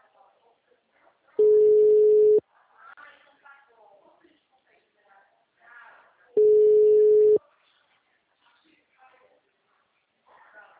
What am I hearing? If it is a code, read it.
Ringback tone (Europe, 425 Hz), with speech.
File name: out- -2016- - - .14968